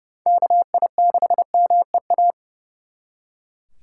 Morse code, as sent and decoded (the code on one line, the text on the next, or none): -.- .. -.... -- . .-
KI6MEA